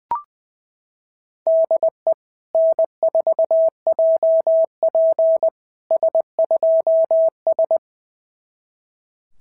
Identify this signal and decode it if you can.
Morse code, heard as a keyed tone, about 20 wpm, 650 Hz. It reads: DE N4JP S2S
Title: SOTA QSO Practice 20wpm